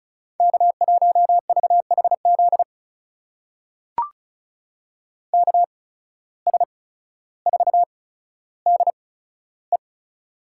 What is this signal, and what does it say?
Morse code, heard as a keyed tone, about 34 wpm, spaced as for 15 wpm, 700 Hz. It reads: K1VHZ KS4DE
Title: USA Call Signs - 5 Characters - 8x Character Spacing - 35wpm